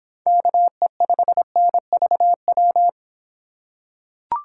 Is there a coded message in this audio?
Yes: Morse code from 0.3 to 2.9 s, heard as a keyed tone, about 26 wpm, 700 Hz. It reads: KE5NVW